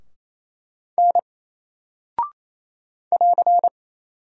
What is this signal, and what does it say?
Morse code, heard as a keyed tone, about 27 wpm, 700 Hz. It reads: N <AR>
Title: Single Character (Letter:Number:Punctuation:Prosign) 28wpm